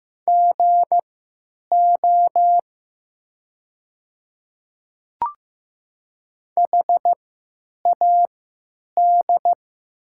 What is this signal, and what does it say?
Morse code, heard as a keyed tone, about 15 wpm, spaced as for 9 wpm, 690 Hz. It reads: GO HAD
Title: 3x Character Spacing - Top 100 Words - 15wpm